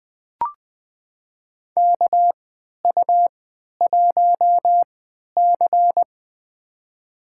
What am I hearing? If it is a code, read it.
Morse code, heard as a keyed tone, about 20 wpm, spaced as for 11 wpm, 700 Hz. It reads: KU1C
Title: USA Call Signs - 4 Characters - 3x Character Spacing - 20wpm